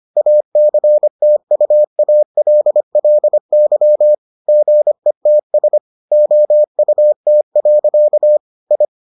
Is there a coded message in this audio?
Yes: Morse code from 0.2 to 8.8 s, heard as a keyed tone, about 25 wpm, 600 Hz. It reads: ACTUALLY GETS OUT. I